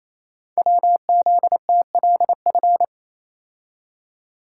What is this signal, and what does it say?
Morse code, heard as a keyed tone, about 29 wpm, 700 Hz. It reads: WZTLF
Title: Code Groups - 5 Letters - 28wpm